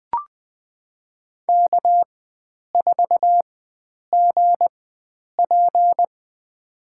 Morse code, heard as a keyed tone, about 20 wpm, spaced as for 9 wpm, 700 Hz. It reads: K4GP
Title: USA Call Signs - 4 Characters - 4x Character Spacing - 20wpm